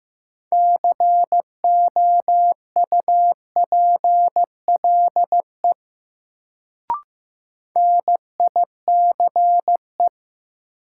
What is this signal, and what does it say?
Morse code, heard as a keyed tone, about 15 wpm, 700 Hz. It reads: COUPLE NICE
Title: Sentences from Top 1100 Words - New Word Drill - 15wpm